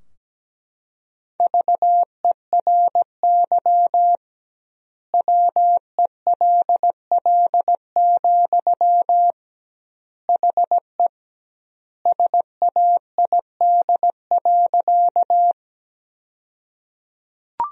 Morse code, heard as a keyed tone, about 17 wpm, 700 Hz. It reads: VERY WELL, HE SAID.
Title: 2x Word-Spacing - Sentences from Top 200 Words - 17wpm